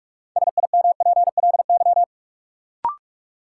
Morse code, heard as a keyed tone, about 46 wpm, 700 Hz. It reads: SIMPLY
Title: Sentences from Top 600 Words - New Word Drill - 45wpm